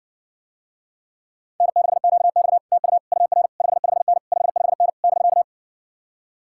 Morse code, heard as a keyed tone, about 60 wpm, 700 Hz. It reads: N6QX TU UR 55N 55N <BK>